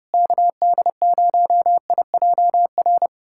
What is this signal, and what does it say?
Morse code, heard as a keyed tone, about 30 wpm, 700 Hz. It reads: KD0IJR
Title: USA Callsign DE Callsign - 6 Characters - 30wpm